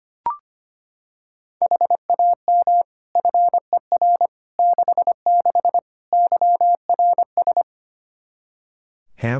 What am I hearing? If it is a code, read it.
Morse code, heard as a keyed tone, about 25 wpm, 700 Hz. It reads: HAM FER 66 YRS
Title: K1USN SST QSO Practice 25wpm